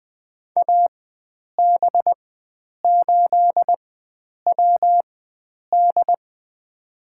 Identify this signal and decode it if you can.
Morse code, heard as a keyed tone, about 20 wpm, spaced as for 9 wpm, 700 Hz. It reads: AB8WD
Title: USA Call Signs - 4x Character Spacing - 20wpm